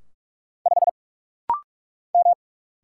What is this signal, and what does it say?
Morse code, heard as a keyed tone, about 44 wpm, 700 Hz. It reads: 5 M
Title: Single Letter-Number Alt - Rapid-Fire 45wpm